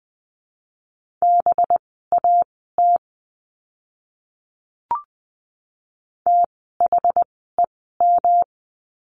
Morse code, heard as a keyed tone, about 20 wpm, spaced as for 14 wpm, 700 Hz. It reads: BAT THEM
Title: 185 Lesson 30f - Letter b - Words 20wpm